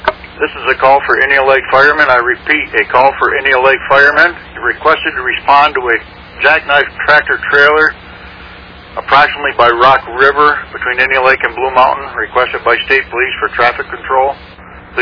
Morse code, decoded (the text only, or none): none